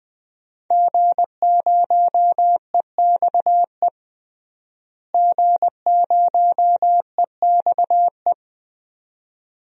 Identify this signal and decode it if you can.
Morse code, heard as a keyed tone, about 20 wpm, 700 Hz. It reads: G0EXE G0EXE
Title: DXCC-100 Call Signs - 5 Characters - Repeated 2x - 20wpm